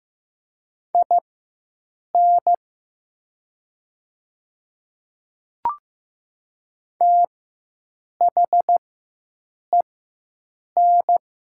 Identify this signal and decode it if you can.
Morse code, heard as a keyed tone, about 15 wpm, spaced as for 7 wpm, 700 Hz. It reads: IN THEN
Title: 4x Character Spacing - Top 100 Words - 15wpm